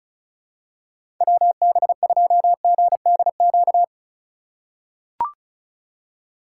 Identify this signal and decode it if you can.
Morse code, heard as a keyed tone, about 36 wpm, 700 Hz. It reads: WB2GDQ
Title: USA Call Signs - 2x Character Spacing - 35wpm